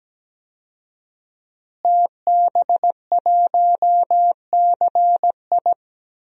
Morse code, heard as a keyed tone, about 17 wpm, 700 Hz. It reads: TB1CI
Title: DXCC-100 Call Signs - 5 Characters - Repeated 3x - 17wpm